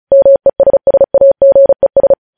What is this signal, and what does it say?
Morse code, heard as a keyed tone, about 36 wpm, 550 Hz. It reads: MESSAGES
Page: News Headlines in Morse Code at 35 WPM